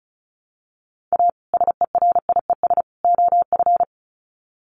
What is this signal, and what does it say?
Morse code, heard as a keyed tone, about 34 wpm, 700 Hz. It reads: A SERIES OF
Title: Sets of 3 Words 35wpm